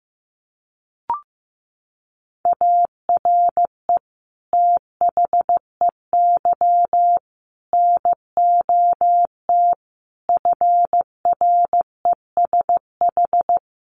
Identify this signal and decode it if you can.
Morse code, heard as a keyed tone, about 15 wpm, 700 Hz. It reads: ARE THEY NOT FRESH
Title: Sentences from Top 1200 Words 15wpm